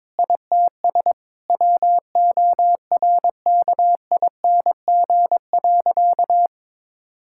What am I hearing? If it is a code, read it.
Morse code, heard as a keyed tone, about 22 wpm, 700 Hz. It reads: ITS WORKING.
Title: Sentences from Top 1000 Words - Review Part 6 - 22wpm